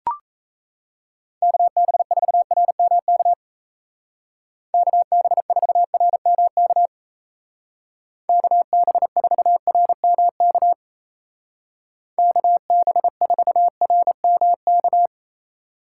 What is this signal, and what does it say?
Morse code, heard as a keyed tone, about 34 wpm, 700 Hz. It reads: KB4RMK KB4RMK KB4RMK KB4RMK
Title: Speed Racing - USA 2x3 Call Signs 28wpm